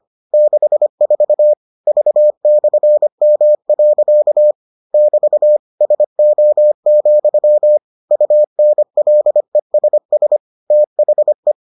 Morse code, heard as a keyed tone, about 25 wpm, 600 Hz. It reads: 64 V/M. <BT> SO, UNLESS THE